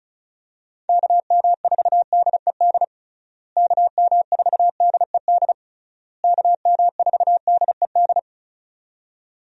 Morse code, heard as a keyed tone, about 34 wpm, 700 Hz. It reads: KM4DED KM4DED KM4DED
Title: USA Call Signs - 6 Characters - Repeated 3x - 35wpm